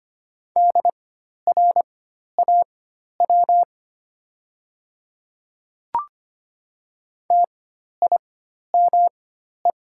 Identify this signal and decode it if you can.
Morse code, heard as a keyed tone, about 25 wpm, spaced as for 12 wpm, 700 Hz. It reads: DRAW TIME
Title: 4x Character Spacing - Top 300 Words - 25wpm